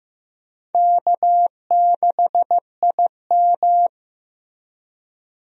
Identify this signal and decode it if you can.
Morse code, heard as a keyed tone, about 15 wpm, 700 Hz. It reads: K6IM
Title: USA Call Signs - 4 Characters - Repeated 3x - 15wpm